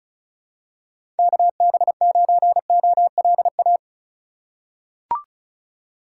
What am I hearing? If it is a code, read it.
Morse code, heard as a keyed tone, about 34 wpm, 700 Hz. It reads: KB9OLA
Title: USA Call Signs - 6 Characters - 4x Character Spacing - 35wpm